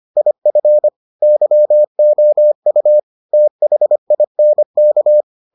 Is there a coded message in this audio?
Yes: Morse code from 0.2 to 5.2 s, heard as a keyed tone, about 25 wpm, 600 Hz. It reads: IF YOU THINK